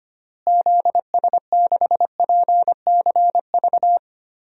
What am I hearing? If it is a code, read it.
Morse code, heard as a keyed tone, about 25 wpm, 700 Hz. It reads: ZS6PCV
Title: DXCC-100 Call Signs - 6 Characters - Repeated 2x - 25wpm